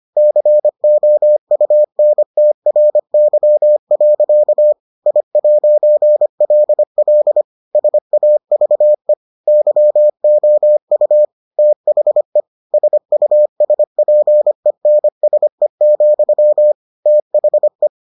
Morse code, heard as a keyed tone, about 25 wpm, 600 Hz. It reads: COUNTRY. I'LL SAVE YOU THE SUSPENSE, THE